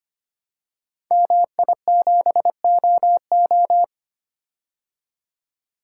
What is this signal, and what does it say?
Morse code, heard as a keyed tone, about 25 wpm, 700 Hz. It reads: MI7OO
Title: Code Groups - 5 Letter:Numbers - 25wpm